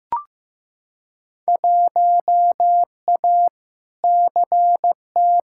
Morse code, heard as a keyed tone, about 16 wpm, 700 Hz. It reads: 1A CT
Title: Field Day Exchanges Encore - 15wpm